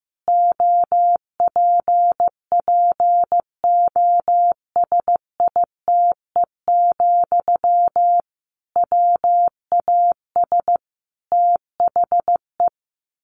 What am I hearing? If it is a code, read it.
Morse code, heard as a keyed tone, about 15 wpm, 700 Hz. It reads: OPPOSITE, WAS THE
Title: Sentences from Top 1300 Words 15wpm